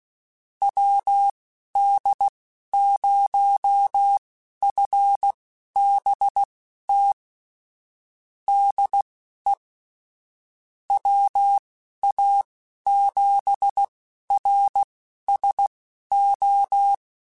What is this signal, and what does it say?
Morse code, heard as a keyed tone, about 16 wpm, spaced as for 12 wpm, 800 Hz. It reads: WD0FBT DE WA7RSO
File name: QSOs / QSOs-mp3s / WD0FBT 1@10